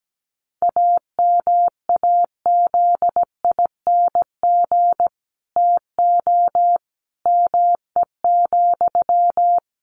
Morse code, heard as a keyed tone, about 17 wpm, 700 Hz. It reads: AMAZING TO ME,
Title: Sentences from Top 1500 Words 17wpm